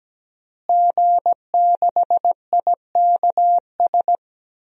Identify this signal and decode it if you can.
Morse code, heard as a keyed tone, about 17 wpm, 700 Hz. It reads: G6IKS
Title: DXCC-100 Call Signs - 5 Characters - Repeated 2x - 17wpm